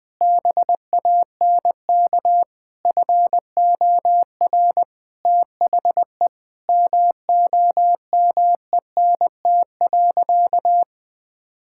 Morse code, heard as a keyed tone, about 20 wpm, 700 Hz. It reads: BANK FOR THE MOMENT.
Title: Sentences from Top 1000 Words - Review Part 6 - 20wpm